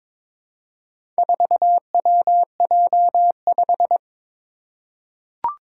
Morse code, heard as a keyed tone, about 22 wpm, 700 Hz. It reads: 4WJ5
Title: Four Challenging Letter-Number Combinations 22wpm